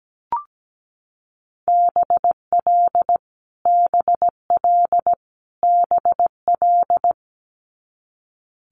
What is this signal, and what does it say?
Morse code, heard as a keyed tone, about 17 wpm, 700 Hz. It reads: BL BL BL
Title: Multi-letter Phonemes Repeated 3x 17wpm